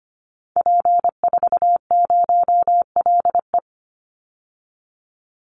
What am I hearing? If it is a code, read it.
Morse code, heard as a keyed tone, about 25 wpm, 700 Hz. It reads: P40LE
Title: Call Signs - 25wpm